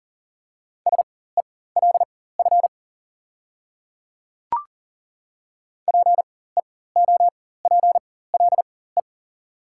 Morse code, heard as a keyed tone, about 40 wpm, spaced as for 19 wpm, 690 Hz. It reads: SELF PEOPLE
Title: 4x Character Spacing - Top 500 Words - 40wpm